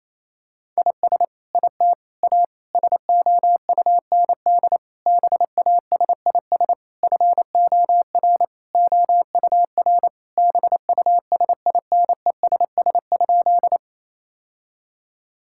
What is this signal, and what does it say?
Morse code, heard as a keyed tone, about 27 wpm, 700 Hz. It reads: IS IT A SOUND BASIS FOR OUR BUSINESS?
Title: Sentences from Top 1000 Words - Review Part 5 - 28wpm